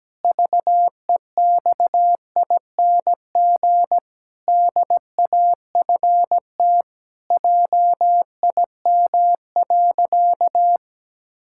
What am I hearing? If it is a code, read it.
Morse code, heard as a keyed tone, about 17 wpm, 700 Hz. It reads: VEXING DAFT JIM.
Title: Pangram Challenge v2 - 17wpm